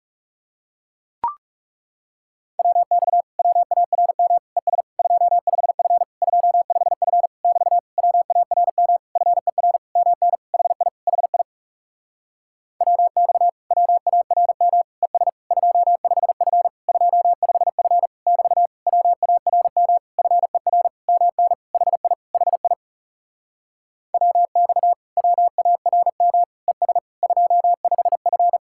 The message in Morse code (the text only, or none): WX WARM ES 25F 25F <BT> WARM FER MN HI HI WX WARM ES 25F 25F <BT> WARM FER MN HI HI WX WARM ES 25F